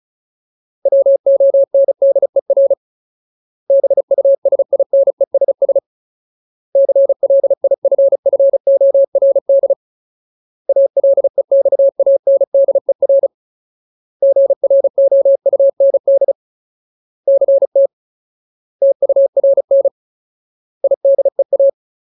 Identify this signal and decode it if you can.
Morse code, heard as a keyed tone, about 34 wpm, 550 Hz. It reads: WONDER BUSINESS CLIFFORD ALEXANDER GROUND CT TURN IDEA
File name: Ultimate Word List / 35WPM Ultimate